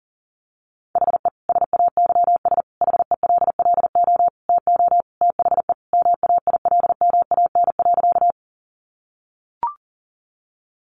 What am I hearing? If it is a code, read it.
Morse code, heard as a keyed tone, about 40 wpm, 700 Hz. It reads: HE SAYS HELLO TO THE MAILMAN.
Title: Easy Phrases 40wpm